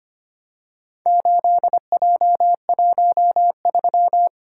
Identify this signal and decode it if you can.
Morse code, heard as a keyed tone, about 25 wpm, 700 Hz. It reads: 8J13
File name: practice / 25wpm / Four Letter-Number Combinations - 25wpm